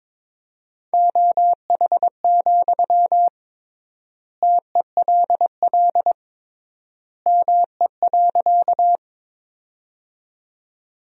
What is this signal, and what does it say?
Morse code, heard as a keyed tone, about 22 wpm, 700 Hz. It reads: OH, TELL ME.
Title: 3x Word-Spacing - Sentences from Top 500 Words - 22wpm